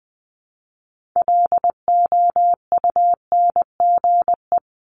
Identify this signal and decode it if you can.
Morse code, heard as a keyed tone, about 19 wpm, 700 Hz. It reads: LOUNGE
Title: Multi-letter Phonemes Repeated 3x 20wpm